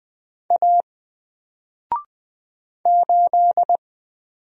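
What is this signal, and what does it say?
Morse code, heard as a keyed tone, about 20 wpm, 700 Hz. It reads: A 8
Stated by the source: Single letter and number Morse drill at 20 wpm, from N4LCD.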